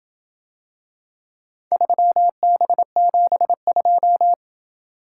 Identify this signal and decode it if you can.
Morse code, heard as a keyed tone, about 27 wpm, 700 Hz. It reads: 3B72